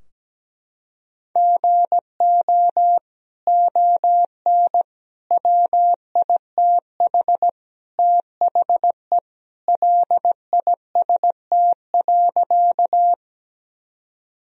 Morse code, heard as a keyed tone, about 17 wpm, 700 Hz. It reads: GO ON WITH THE LIST.
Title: Sentences from Top 1000 Words - Review Part 2 - 17wpm